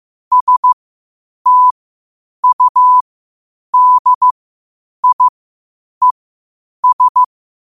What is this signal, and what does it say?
Morse code, heard as a keyed tone, about 15 wpm, spaced as for 8 wpm, 1000 Hz. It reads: STUDIES
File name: qotdfiles / 8wpm